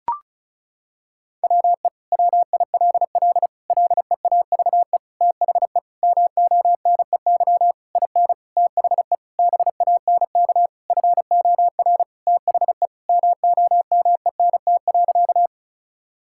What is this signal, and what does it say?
Morse code, heard as a keyed tone, about 36 wpm, 700 Hz. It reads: WE WILL LEAVE THE MONEY IN THE BANK FOR THE MOMENT.
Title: Sentences from Top 1000 Words - Review Part 6 - 35wpm